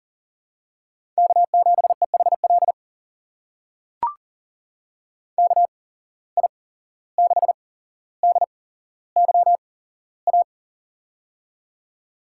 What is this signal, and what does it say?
Morse code, heard as a keyed tone, about 43 wpm, spaced as for 16 wpm, 690 Hz. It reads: K7EHL KI6DYA
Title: USA Call Signs - 8x Character Spacing - 40wpm